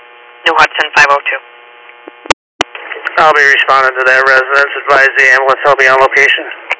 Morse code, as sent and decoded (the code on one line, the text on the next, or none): none